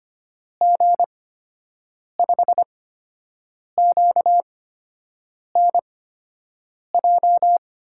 Morse code, heard as a keyed tone, about 25 wpm, spaced as for 7 wpm, 700 Hz. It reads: G5QNJ
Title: USA Call Signs - 6 Characters - 8x Character Spacing - 25wpm